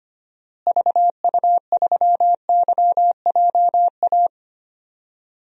Morse code, heard as a keyed tone, about 25 wpm, 700 Hz. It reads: VU3YJA
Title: DXCC-100 Call Signs - 6 Characters - Repeated 2x - 25wpm